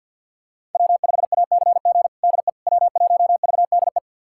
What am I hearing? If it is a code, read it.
Morse code, heard as a keyed tone, about 52 wpm, 700 Hz. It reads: W5ACG DE W1VDE
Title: USA Callsign DE Callsign - 5 Characters - 50wpm